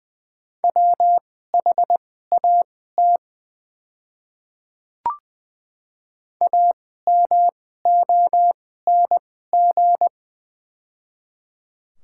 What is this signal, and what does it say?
Morse code, heard as a keyed tone, about 21 wpm, spaced as for 15 wpm, 700 Hz. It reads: WHAT AMONG
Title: 2x Character Spacing - Top 500 Words - 20wpm